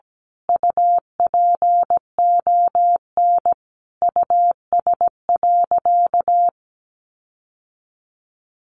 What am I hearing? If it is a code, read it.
Morse code, heard as a keyed tone, about 17 wpm, 700 Hz. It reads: UPON US.
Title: Sentences from Top 1500 Words 17wpm